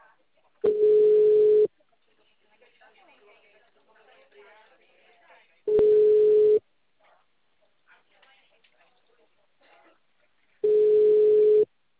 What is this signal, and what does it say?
Ringback tone (Europe, 425 Hz), with speech.